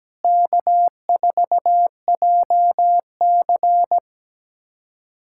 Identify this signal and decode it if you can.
Morse code, heard as a keyed tone, about 17 wpm, 700 Hz. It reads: K4JC
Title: USA Callsign DE Callsign - 4 Characters - 17wpm